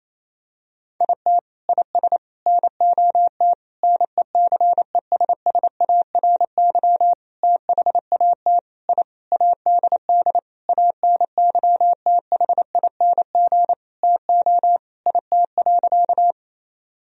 Morse code, heard as a keyed tone, about 28 wpm, 700 Hz. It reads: IT IS NOT NECESSARY THAT I ADD ANYTHING TO IT.
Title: Sentences from Top 1000 Words - Review Part 1 - 28wpm